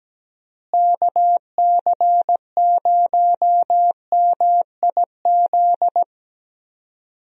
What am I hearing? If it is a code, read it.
Morse code, heard as a keyed tone, about 17 wpm, 700 Hz. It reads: KC0MIZ
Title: USA Call Signs - 6 Characters - Repeated 2x - 17wpm